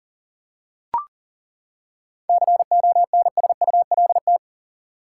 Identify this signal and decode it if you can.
Morse code, heard as a keyed tone, about 40 wpm, 700 Hz. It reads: CONSULT
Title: Seven Letter Words - 40wpm